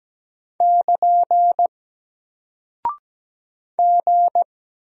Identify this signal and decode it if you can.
Morse code, heard as a keyed tone, about 17 wpm, 700 Hz. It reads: <KN> G
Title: Single Character (Letter:Number:Punctuation:Prosign) Alt 17wpm